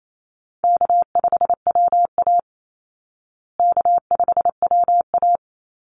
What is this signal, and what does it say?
Morse code, heard as a keyed tone, about 28 wpm, 700 Hz. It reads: K5WA K5WA
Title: Easy Call Signs Repeated 3x - 28wpm